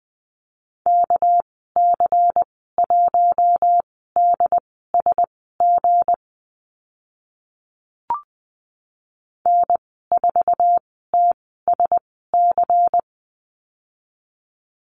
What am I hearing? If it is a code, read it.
Morse code, heard as a keyed tone, about 20 wpm, spaced as for 14 wpm, 700 Hz. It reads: KC1DSG N4TSC
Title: 213 Lesson 34g - Number 8 - Call Signs 20wpm